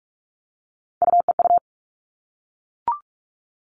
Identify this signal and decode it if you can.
Morse code, heard as a keyed tone, about 44 wpm, 700 Hz. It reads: UEU
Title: All Cut Numbers - 3-Digit 45wpm